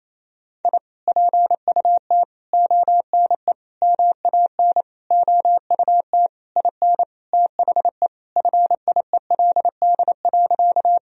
Morse code, heard as a keyed tone, about 28 wpm, 700 Hz. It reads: I PUT ONE MAN OUT IN THE FIELD.
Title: Sentences from Top 1000 Words - Review Part 2 - 28wpm